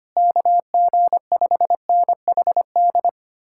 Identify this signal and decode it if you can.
Morse code, heard as a keyed tone, about 32 wpm, 700 Hz. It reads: KG5NHD